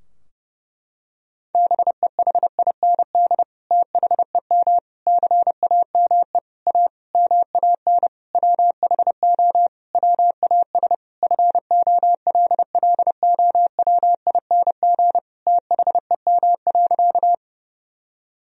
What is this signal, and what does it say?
Morse code, heard as a keyed tone, about 32 wpm, 700 Hz. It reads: BEHIND THEM CAME A MAN WHO WAS FOLLOWING THEM.
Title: Sentences from Top 1000 Words - Review Part 4 - 30wpm